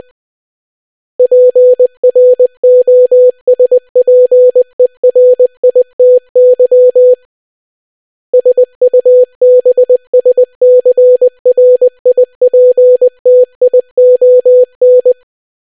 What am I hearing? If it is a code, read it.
Morse code, heard as a keyed tone, about 20 wpm, 500 Hz. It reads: PROSPERITY SUBSCRIPTION